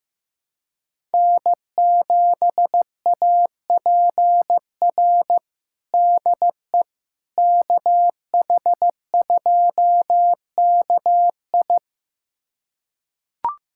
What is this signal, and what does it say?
Morse code, heard as a keyed tone, about 15 wpm, 700 Hz. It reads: N7APR DE KH2KI
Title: USA Callsign DE Callsign - 5 Characters - 15wpm